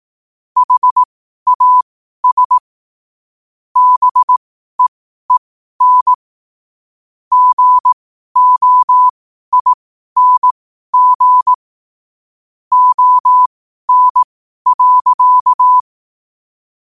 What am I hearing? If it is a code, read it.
Morse code, heard as a keyed tone, about 18 wpm, spaced as for 12 wpm, 1000 Hz. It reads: HAS BEEN GOING ON.
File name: qotdfiles / 12wpm